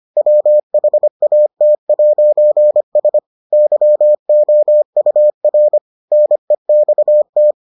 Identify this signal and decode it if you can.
Morse code, heard as a keyed tone, about 25 wpm, 600 Hz. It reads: WHAT'S YOUR NEXT